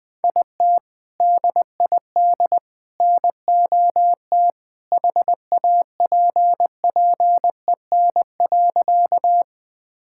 Morse code, heard as a keyed tone, about 20 wpm, 700 Hz. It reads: IT DID NOT HAPPEN.
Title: Sentences with 3-Word Phrases - Spoken Beginning - 20wpm